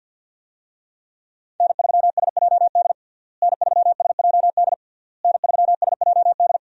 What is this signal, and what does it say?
Morse code, heard as a keyed tone, about 50 wpm, 700 Hz. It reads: N3SJD N3SJD N3SJD
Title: USA Call Signs - 5 Characters - Repeated 3x - 50wpm